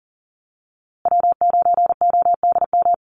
Morse code, heard as a keyed tone, about 40 wpm, 700 Hz. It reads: W9ODM
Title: USA Call Signs Repeated 3x 40wpm